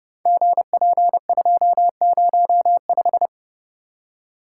Morse code, heard as a keyed tone, about 30 wpm, 700 Hz. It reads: GP205
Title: Code Groups - 5 Letter:Numbers - 30wpm